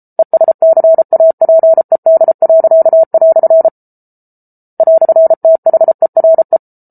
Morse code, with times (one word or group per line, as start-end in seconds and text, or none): 0.2-3.7 ESCAPED."
4.8-6.6 "THERE